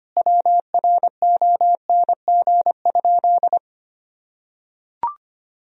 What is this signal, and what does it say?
Morse code, heard as a keyed tone, about 25 wpm, 700 Hz. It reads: WRONG?
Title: Sentences from Top 1000 Words - Review Part 1 - 25wpm